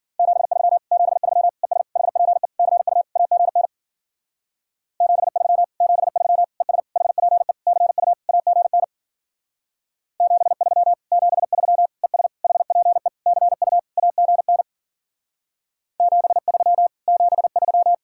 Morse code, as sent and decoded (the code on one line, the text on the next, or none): --... ...-- / --... ...-- / . ... / .... .--. / -.-. ..- / .- --. -. / --... ...-- / --... ...-- / . ... / .... .--. / -.-. ..- / .- --. -. / --... ...-- / --... ...-- / . ... / .... .--. / -.-. ..- / .- --. -. / --... ...-- / --... ...--
73 73 ES HPE CU AGN 73 73 ES HPE CU AGN 73 73 ES HPE CU AGN 73 73